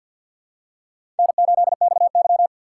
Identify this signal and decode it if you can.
Morse code, heard as a keyed tone, about 52 wpm, 700 Hz. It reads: N8XY
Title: USA Call Signs - 4 Characters - 2x Character Spacing - 50wpm